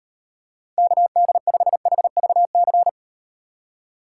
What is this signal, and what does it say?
Morse code, heard as a keyed tone, about 38 wpm, 700 Hz. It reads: KD5HVC